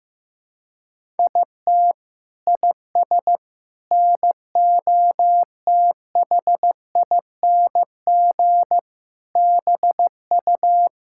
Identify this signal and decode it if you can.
Morse code, heard as a keyed tone, about 15 wpm, 700 Hz. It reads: IT IS NOTHING BU
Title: Sentences from Top 1600 Words 15wpm